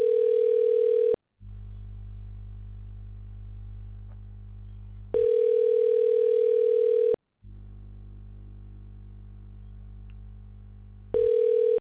Ringback tone (North America), 440 + 480 Hz.